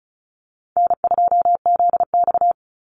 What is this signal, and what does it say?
Morse code, heard as a keyed tone, about 34 wpm, 700 Hz. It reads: N2ZX
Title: Call Signs Repeated 3x - 35wpm